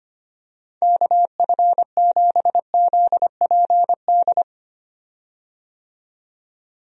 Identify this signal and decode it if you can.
Morse code, heard as a keyed tone, about 25 wpm, 700 Hz. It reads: KF7ZPD
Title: USA Call Signs - 6 Characters - 25wpm